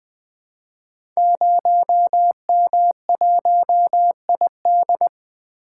Morse code, heard as a keyed tone, about 20 wpm, 700 Hz. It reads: 0M1ID
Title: Code Groups - 5 Letter:Numbers - 20wpm